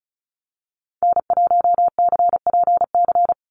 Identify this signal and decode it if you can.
Morse code, heard as a keyed tone, about 34 wpm, 700 Hz. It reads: N1CPC
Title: USA Call Signs 35wpm